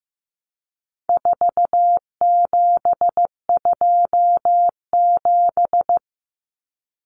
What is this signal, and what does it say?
Morse code, heard as a keyed tone, about 15 wpm, 700 Hz. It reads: 4727
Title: Numbers - 4-Digit 15wpm